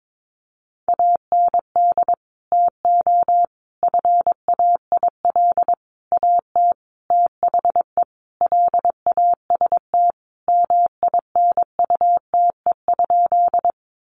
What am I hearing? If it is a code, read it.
Morse code, heard as a keyed tone, about 22 wpm, 700 Hz. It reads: AND TO FAIL AT THE LAST MINUTE?
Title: Sentences from Top 1100 Words 22wpm